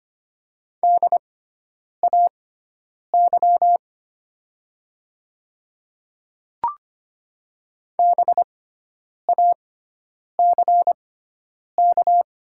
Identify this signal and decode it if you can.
Morse code, heard as a keyed tone, about 25 wpm, spaced as for 9 wpm, 700 Hz. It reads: DAY BACK
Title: 6x Character Spacing - Top 200 Words - 25wpm